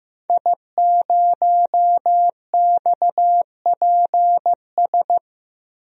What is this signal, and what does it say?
Morse code, heard as a keyed tone, about 15 wpm, 700 Hz. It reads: I0XPS
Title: DXCC-100 Call Signs - 5 Characters - Repeated 2x - 15wpm